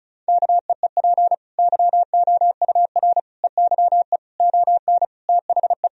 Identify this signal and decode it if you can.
Morse code, heard as a keyed tone, about 36 wpm, 700 Hz. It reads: KEEP YOUR EYE ON THE
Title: Sentences from Top 1000 Words - Review Part 4 - 35wpm